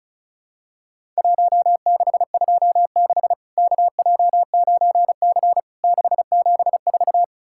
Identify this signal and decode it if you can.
Morse code, heard as a keyed tone, about 36 wpm, 700 Hz. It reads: 1626 KJ9C 674
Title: Worked All Europe DX Contest - QTC Exchange - 35wpm